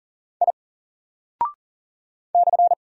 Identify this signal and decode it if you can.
Morse code, heard as a keyed tone, about 39 wpm, 700 Hz. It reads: I /
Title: Single Character (Letter:Number:Punctuation:Prosign) Alt 40wpm